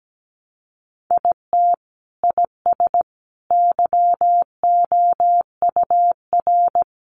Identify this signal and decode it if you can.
Morse code, heard as a keyed tone, about 17 wpm, 700 Hz. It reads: IT IS YOUR
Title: Sentences from Top 800 Words 17wpm